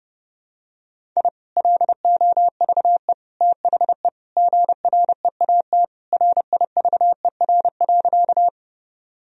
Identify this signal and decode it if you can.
Morse code, heard as a keyed tone, about 30 wpm, 700 Hz. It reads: I LOVE THE GREAT RIVER.
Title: Sentences from Top 1000 Words - Review Part 1 - 30wpm